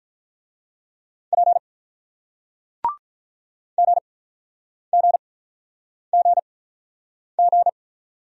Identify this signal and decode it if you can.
Morse code, heard as a keyed tone, about 48 wpm, 700 Hz. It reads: P G G G G